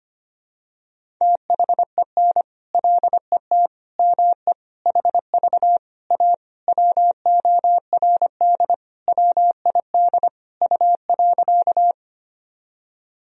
Morse code, heard as a keyed tone, about 25 wpm, 700 Hz. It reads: THEN LET ME HV A WORD WID U.
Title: Sentences from Top 300 Words and Abbreviations 25wpm